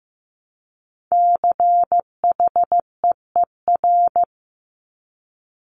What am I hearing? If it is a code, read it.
Morse code, heard as a keyed tone, about 15 wpm, 700 Hz. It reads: CHEER
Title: Multi-letter Phonemes Repeated 3x 15wpm